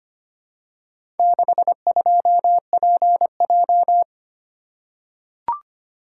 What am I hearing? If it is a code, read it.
Morse code, heard as a keyed tone, about 25 wpm, 700 Hz. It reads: 62PJ
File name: practice / 25wpm / Four Letter-Number Combinations - 25wpm